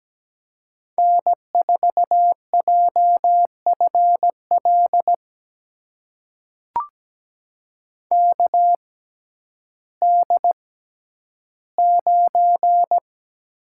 Morse code, heard as a keyed tone, about 17 wpm, spaced as for 10 wpm, 700 Hz. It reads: N4JFL KD9
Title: USA Call Signs - 6x Character Spacing - 17wpm